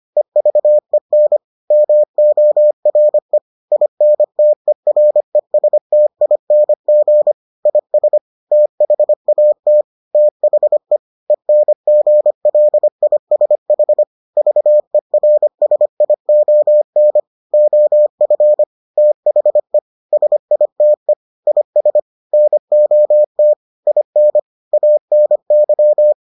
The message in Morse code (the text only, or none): EVEN MORE INTERESTING IS THAT THE ENGLISH VERSION OF THE SITE IS NOT IN ANY